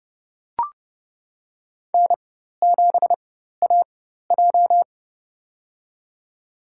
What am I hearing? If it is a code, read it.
Morse code, heard as a keyed tone, about 30 wpm, spaced as for 14 wpm, 700 Hz. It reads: N7AJ
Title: USA Call Signs - 4x Character Spacing - 30wpm